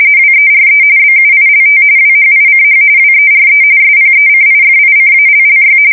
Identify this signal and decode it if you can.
RTTY, about 45.45 baud. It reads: WERE BUILT WHICH WERE SPECIFICALLY
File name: fsk170 / usnavy4